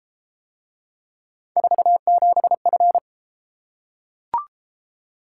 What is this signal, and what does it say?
Morse code, heard as a keyed tone, about 28 wpm, 700 Hz. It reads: 47F